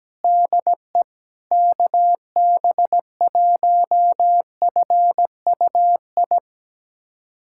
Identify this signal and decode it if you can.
Morse code, heard as a keyed tone, about 17 wpm, 700 Hz. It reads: DE KB1FUI
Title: USA Callsign DE Callsign - 6 Characters - 17wpm